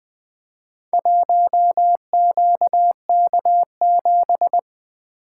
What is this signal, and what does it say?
Morse code, heard as a keyed tone, about 20 wpm, 700 Hz. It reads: 1QK7
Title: Four Letter-Number Combinations - 20wpm